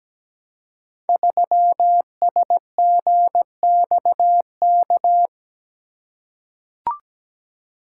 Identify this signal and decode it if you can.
Morse code, heard as a keyed tone, about 17 wpm, 700 Hz. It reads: 3SGXK
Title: Code Groups - 5 Letter:Numbers - 17wpm